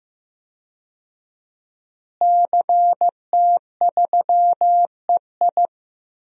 Morse code, heard as a keyed tone, about 15 wpm, 700 Hz. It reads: CT3EI
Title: DXCC-100 Call Signs - 5 Characters - Repeated 2x - 15wpm